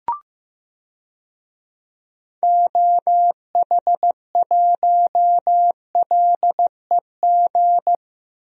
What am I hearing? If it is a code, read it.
Morse code, heard as a keyed tone, about 15 wpm, 700 Hz. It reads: OH1LEG
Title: DXCC-100 Call Signs - 6 Characters - 15wpm